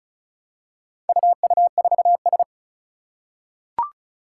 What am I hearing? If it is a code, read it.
Morse code, heard as a keyed tone, about 36 wpm, 700 Hz. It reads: UU4S
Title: Four Challenging Letter-Number Combinations 35wpm